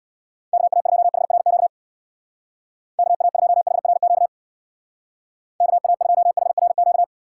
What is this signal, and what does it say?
Morse code, heard as a keyed tone, about 62 wpm, 700 Hz. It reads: KA2HRQ KA2HRQ KA2HRQ
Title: Speed Racing - USA 2x3 Call Signs 50wpm